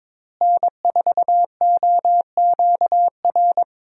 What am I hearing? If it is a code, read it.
Morse code, heard as a keyed tone, about 22 wpm, 700 Hz. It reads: N4OQR
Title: USA Call Signs - 5 Characters - Repeated 2x - 22wpm